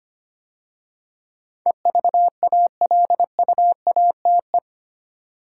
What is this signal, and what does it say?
Morse code, heard as a keyed tone, about 25 wpm, 700 Hz. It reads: EVALUATE
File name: 8 Letter Words - First 3 Letters Given - 25wpm